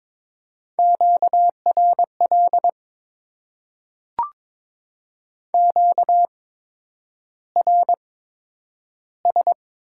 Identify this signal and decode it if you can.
Morse code, heard as a keyed tone, about 22 wpm, spaced as for 9 wpm, 700 Hz. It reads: QRL QRS
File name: Top 90 QSO Elements - 8x Character Spacing - 22wpm